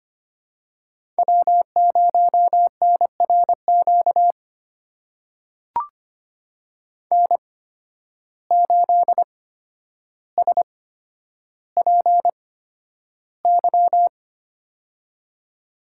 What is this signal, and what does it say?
Morse code, heard as a keyed tone, about 25 wpm, spaced as for 11 wpm, 700 Hz. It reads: W0NRQ N8SPY
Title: USA Call Signs - 5 Characters - 8x Character Spacing - 25wpm